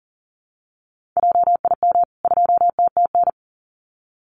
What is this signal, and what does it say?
Morse code, heard as a keyed tone, about 40 wpm, 700 Hz. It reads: JIM 2TTN
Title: CWT Exchange 40wpm